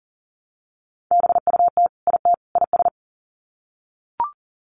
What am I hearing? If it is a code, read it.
Morse code, heard as a keyed tone, about 40 wpm, 700 Hz. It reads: BUT IT IS
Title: Sets of 3 Words 40wpm